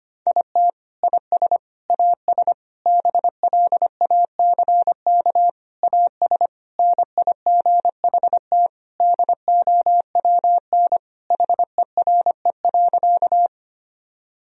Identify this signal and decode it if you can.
Morse code, heard as a keyed tone, about 25 wpm, 700 Hz. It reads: IT IS AS BLACK AS NIGHT DOWN HERE.
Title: Sentences from Top 1000 Words - Review Part 2 - 25wpm